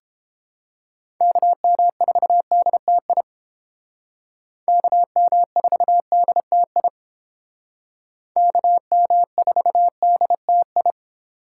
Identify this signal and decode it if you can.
Morse code, heard as a keyed tone, about 29 wpm, 700 Hz. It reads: KM4DTI KM4DTI KM4DTI